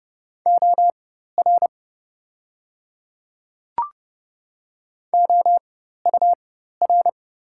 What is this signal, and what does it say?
Morse code, heard as a keyed tone, about 30 wpm, spaced as for 14 wpm, 690 Hz. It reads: OR OUR
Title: 4x Character Spacing - Top 200 Words - 30wpm